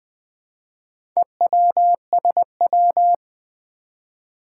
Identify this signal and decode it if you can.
Morse code, heard as a keyed tone, about 20 wpm, 700 Hz. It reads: EWSW
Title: Four Challenging Letter-Number Combinations 20wpm